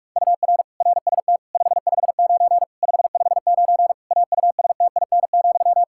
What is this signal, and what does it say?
Morse code, heard as a keyed tone, about 41 wpm, 700 Hz. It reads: UR RST 559 559 AUSTIN,
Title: K1USN SST QSO Practice 45wpm